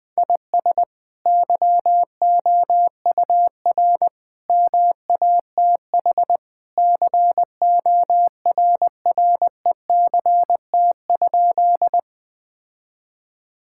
Morse code, heard as a keyed tone, about 21 wpm, 700 Hz. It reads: IS YOUR MATH CORRECT?
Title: Sentences from Top 1000 Words - Review Part 6 - 20wpm